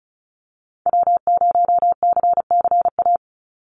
Morse code, heard as a keyed tone, about 36 wpm, 700 Hz. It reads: W0CCA
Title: NA SOTA Activator Callsigns 35wpm